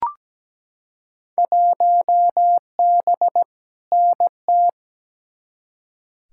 Morse code, heard as a keyed tone, about 17 wpm, 700 Hz. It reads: 1B NT
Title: Field Day Exchanges Encore - 17wpm